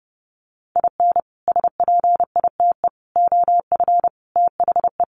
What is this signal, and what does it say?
Morse code, heard as a keyed tone, about 30 wpm, 700 Hz. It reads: IN SPITE OF THE
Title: Sets of 5 Words Encore2 30wpm